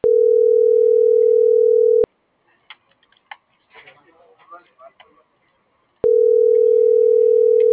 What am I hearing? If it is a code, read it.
Ringback tone (North America), 440 + 480 Hz.